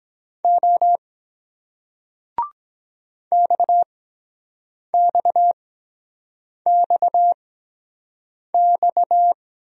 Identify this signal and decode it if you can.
Morse code, heard as a keyed tone, about 23 wpm, spaced as for 7 wpm, 700 Hz. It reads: O XXXX